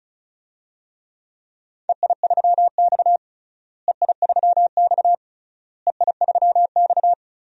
Morse code, heard as a keyed tone, about 35 wpm, 700 Hz. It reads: EI3X EI3X EI3X
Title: DXCC-100 Call Signs - 4 Characters - Repeated 3x - 35wpm